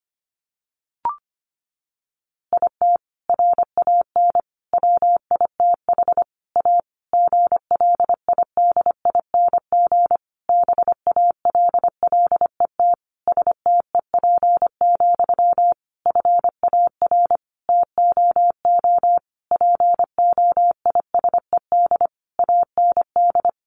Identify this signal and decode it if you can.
Morse code, heard as a keyed tone, about 25 wpm, 700 Hz. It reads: IT RAN WITH A GLIDING BALLET STEP, FAR TOO POISED AND
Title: A Sound Of Thunder 25wpm